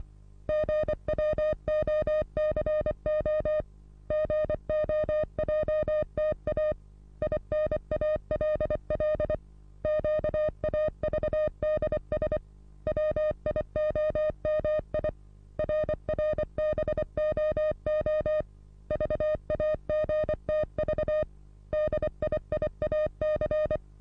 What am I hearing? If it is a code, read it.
Morse code, heard as a keyed tone, about 25 wpm, 610 Hz. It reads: GWOCO GOJTA INALL QAVDS WIOMI RRBOO VAGTV DIIAC
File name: repos / Morsekurs / DJ4UF Morsen27